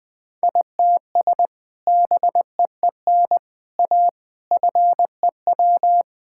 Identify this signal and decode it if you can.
Morse code, heard as a keyed tone, about 21 wpm, 700 Hz. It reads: ITS BEEN A FEW
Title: Sentences with 3-Word Phrases - Spoken Beginning - 20wpm